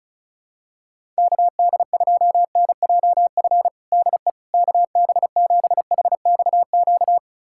Morse code, heard as a keyed tone, about 36 wpm, 700 Hz. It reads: KD2NJF DE KB7HXQ
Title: USA Callsign DE Callsign - 6 Characters - 35wpm